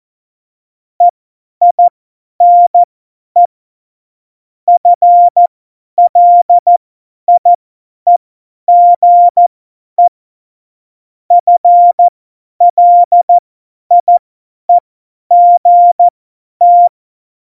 Morse code, heard as a keyed tone, about 14 wpm, spaced as for 10 wpm, 700 Hz. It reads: EINE FLIEGE FLIEGT
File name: Morsekurs / Witze / Witz 24 April 2018, 50 F70